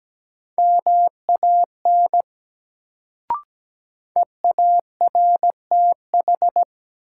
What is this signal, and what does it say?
Morse code, heard as a keyed tone, about 17 wpm, 700 Hz. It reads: MAN EARTH
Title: Sentences from Top 200 Words - New Word Drill - 17wpm